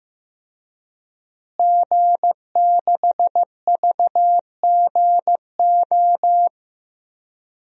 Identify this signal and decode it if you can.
Morse code, heard as a keyed tone, about 15 wpm, 700 Hz. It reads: G6VGO
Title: DXCC-100 Call Signs - 5 Characters - Repeated 2x - 15wpm